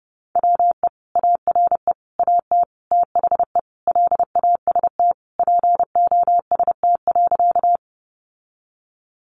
Morse code, heard as a keyed tone, about 30 wpm, 700 Hz. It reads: WE ARE AT THE LAST POST.
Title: Sentences from Top 1000 Words 30wpm